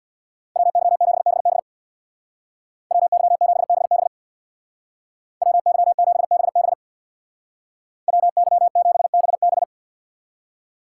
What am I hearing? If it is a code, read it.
Morse code, heard as a keyed tone, about 64 wpm, 700 Hz. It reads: WY7BB WY7BB WY7BB WY7BB